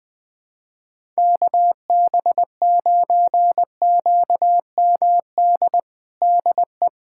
Morse code, heard as a keyed tone, about 20 wpm, 700 Hz. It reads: KB9QMD DE
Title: USA Callsign DE Callsign 20wpm